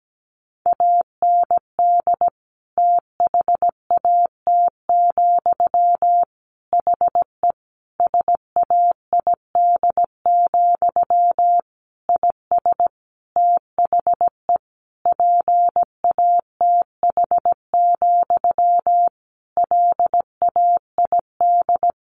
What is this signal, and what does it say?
Morse code, heard as a keyed tone, about 17 wpm, 700 Hz. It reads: AND THAT, HE SAID, IS THE PATH, LAID
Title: A Sound Of Thunder 17wpm